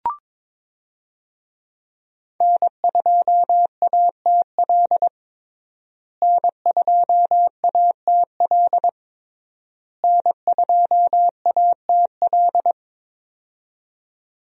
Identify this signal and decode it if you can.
Morse code, heard as a keyed tone, about 22 wpm, 700 Hz. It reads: N2ATL N2ATL N2ATL
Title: USA Call Signs - 5 Characters - Repeated 3x - 22wpm